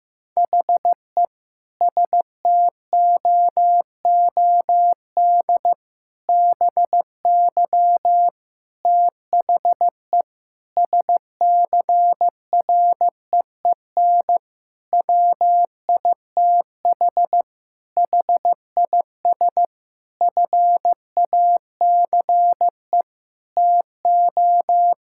Morse code, heard as a keyed tone, about 15 wpm, 700 Hz. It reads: HE STOOD BY THE SCREEN WITH HIS FACE TO
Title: Sentences from Top 1000 Words - Review Part 6 - 15wpm